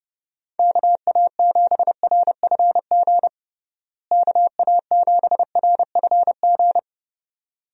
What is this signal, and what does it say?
Morse code, heard as a keyed tone, about 32 wpm, 700 Hz. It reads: KA7RFG KA7RFG
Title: USA Call Signs - 6 Characters - Repeated 2x - 30wpm